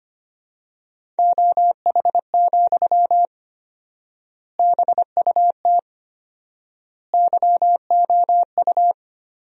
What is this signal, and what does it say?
Morse code, heard as a keyed tone, about 25 wpm, 700 Hz. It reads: OH, BUT YOU
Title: 4x Word-Spacing - Sentences from Top 500 Words - 25wpm